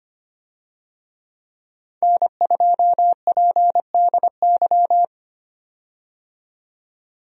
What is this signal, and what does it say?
Morse code, heard as a keyed tone, about 25 wpm, 700 Hz. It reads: N2PDY
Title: USA Call Signs - 5 Characters - 25wpm